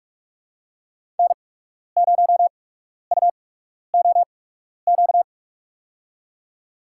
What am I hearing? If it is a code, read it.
Morse code, heard as a keyed tone, about 43 wpm, spaced as for 12 wpm, 700 Hz. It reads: N0UOQ
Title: USA Call Signs - 8x Character Spacing - 45wpm